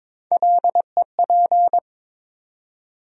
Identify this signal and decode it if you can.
Morse code, heard as a keyed tone, about 22 wpm, 700 Hz. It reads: LEP
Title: USA Call Sign Postfixes - 3 Letters 22wpm